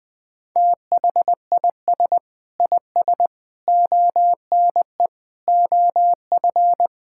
Morse code, heard as a keyed tone, about 20 wpm, 700 Hz. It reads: THIS IS ONE OF
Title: Sentences with 4-Word Phrases - 20wpm